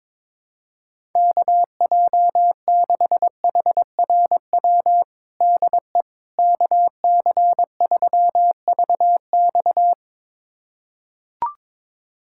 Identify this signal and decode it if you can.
Morse code, heard as a keyed tone, about 22 wpm, 700 Hz. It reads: KJ6HRW DE KC3VX
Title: USA Callsign DE Callsign 22wpm